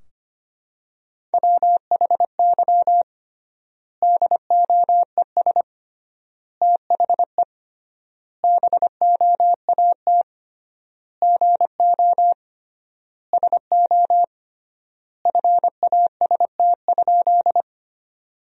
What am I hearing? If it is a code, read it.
Morse code, heard as a keyed tone, about 25 wpm, 700 Hz. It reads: WHY DOES THE BOAT GO SO FAST?
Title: 3x Word-Spacing - Sentences from Top 500 Words - 25wpm